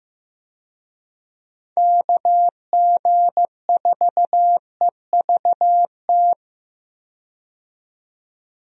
Morse code, heard as a keyed tone, about 15 wpm, 700 Hz. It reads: KG4EVT
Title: USA Call Signs - 6 Characters - 15wpm